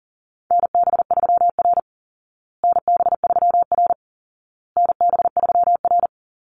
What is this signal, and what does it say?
Morse code, heard as a keyed tone, about 40 wpm, 700 Hz. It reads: NB3R NB3R NB3R
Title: Call Signs Repeated 2x - 40wpm